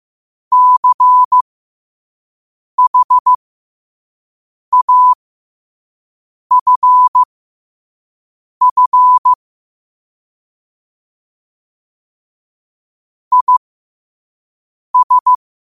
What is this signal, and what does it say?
Morse code, heard as a keyed tone, about 15 wpm, spaced as for 5 wpm, 1000 Hz. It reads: CHAFF IS
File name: qotdfiles / 5wpm